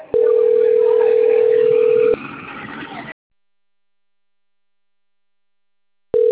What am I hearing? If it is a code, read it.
Ringback tone (North America), 440 + 480 Hz.